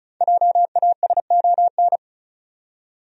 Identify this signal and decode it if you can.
Morse code, heard as a keyed tone, about 36 wpm, 700 Hz. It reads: JASON
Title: USA Common Names - 35wpm